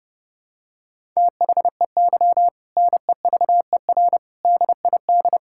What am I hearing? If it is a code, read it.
Morse code, heard as a keyed tone, about 32 wpm, 700 Hz. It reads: THEY NEVER DID
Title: Sentences from Top 1000 Words - Review Part 5 - 30wpm